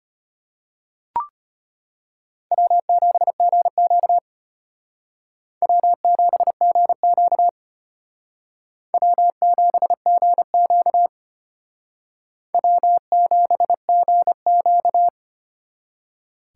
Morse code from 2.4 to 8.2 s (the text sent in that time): W7GQ W7GQ